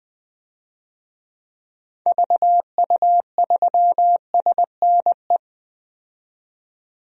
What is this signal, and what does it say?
Morse code, heard as a keyed tone, about 21 wpm, 700 Hz. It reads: VU3SNE
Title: DXCC-100 Call Signs - 6 Characters - 20wpm